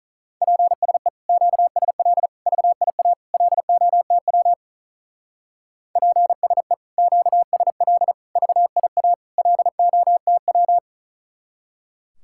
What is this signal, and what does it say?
Morse code, heard as a keyed tone, about 43 wpm, 700 Hz. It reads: PSE QSL VIA LOTW PSE QSL VIA LOTW
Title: Speed-Racing Common QSO Phrases - 35wpm